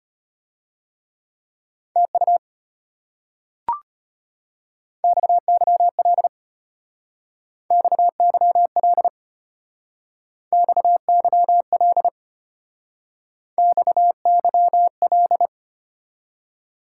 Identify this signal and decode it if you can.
Morse code, heard as a keyed tone, about 32 wpm, 700 Hz. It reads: TU XYL XYL XYL XYL